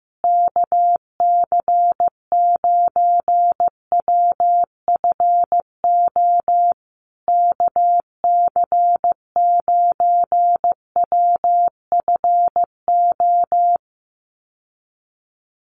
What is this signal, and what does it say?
Morse code, heard as a keyed tone, about 15 wpm, 700 Hz. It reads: KC9WFO KC9WFO
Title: USA Call Signs Repeated 3x 15wpm